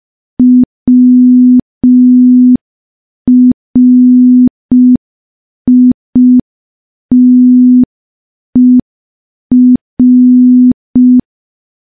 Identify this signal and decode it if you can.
Morse code, heard as a keyed tone, about 5 wpm, 250 Hz. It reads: WRITER
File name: morse / morse-training-05WPM